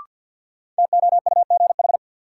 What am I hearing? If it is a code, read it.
Morse code, heard as a keyed tone, about 51 wpm, 700 Hz. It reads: TOUGH
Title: Sentences from Top 1200 Words - New Word Drill - 50wpm